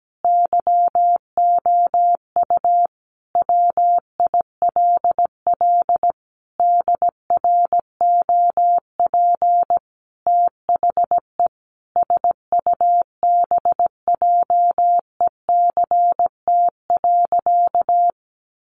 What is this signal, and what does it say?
Morse code, heard as a keyed tone, about 17 wpm, 700 Hz. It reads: YOU WILL DROP THE SUBJECT.
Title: Sentences from Top 1400 Words 17wpm